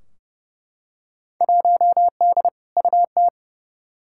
Morse code, heard as a keyed tone, about 31 wpm, 700 Hz. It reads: 1D UT
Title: Field Day Exchanges - 30wpm